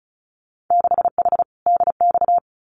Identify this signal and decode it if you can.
Morse code, heard as a keyed tone, about 36 wpm, 700 Hz. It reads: 6H DX
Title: Winter Field Day Exchanges Encore 35wpm